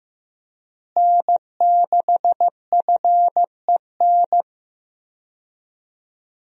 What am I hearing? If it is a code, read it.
Morse code, heard as a keyed tone, about 15 wpm, 700 Hz. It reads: N6FEN
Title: USA Call Signs - 5 Characters - Repeated 3x - 15wpm